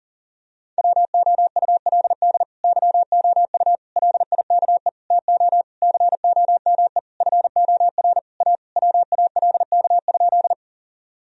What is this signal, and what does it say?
Morse code, heard as a keyed tone, about 40 wpm, 700 Hz. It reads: WOULD YOU LIKE TO COME FOR A WALK?
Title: Sentences from Top 1000 Words - Review Part 1 - 40wpm